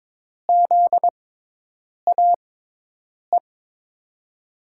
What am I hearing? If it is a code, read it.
Morse code, heard as a keyed tone, about 22 wpm, spaced as for 8 wpm, 700 Hz. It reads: ZAE
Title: USA Call Signs - 6 Characters - 6x Character Spacing - 22wpm